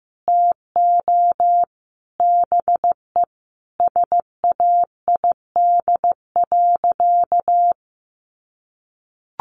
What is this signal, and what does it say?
Morse code, heard as a keyed tone, about 15 wpm, 700 Hz. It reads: TO BE SAID.
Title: Sentences from Top 100 Words 15wpm